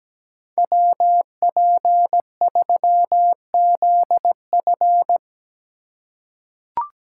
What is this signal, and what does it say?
Morse code, heard as a keyed tone, about 17 wpm, 700 Hz. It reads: WP3ZF
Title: USA Call Signs - 5 Characters - 4x Character Spacing - 17wpm